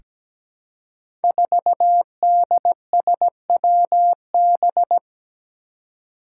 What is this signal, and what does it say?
Morse code, heard as a keyed tone, about 17 wpm, 700 Hz. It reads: 4DSWB
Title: Five Challenging Letter-Number Combinations 17wpm